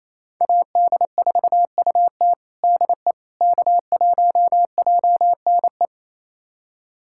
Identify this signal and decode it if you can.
Morse code, heard as a keyed tone, about 27 wpm, 700 Hz. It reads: AD4UT DE K1JNE
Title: USA Callsign DE Callsign - 5 Characters - 28wpm